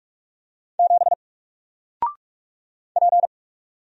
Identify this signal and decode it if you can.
Morse code, heard as a keyed tone, about 44 wpm, 700 Hz. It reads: 7 P
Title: Single Character (Letter:Number:Punctuation:Prosign) 45wpm